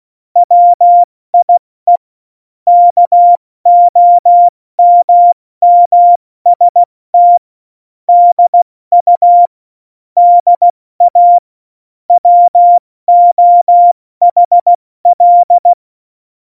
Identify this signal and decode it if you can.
Morse code, heard as a keyed tone, about 16 wpm, 700 Hz. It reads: WIE KOMMST DU DA WOHL
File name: Morsekurs / Witze / Witz 30. Oktober 2018, 70 F80